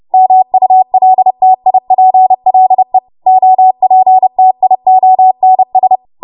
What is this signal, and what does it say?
Morse code, heard as a keyed tone, about 30 wpm, 750 Hz. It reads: MULTIPLE OPTIONS
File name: Morse / 30WPM